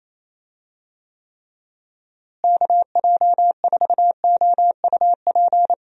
Morse code, heard as a keyed tone, about 28 wpm, 700 Hz. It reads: KJ4OUP
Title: USA Call Signs - 6 Characters - Repeated 3x - 28wpm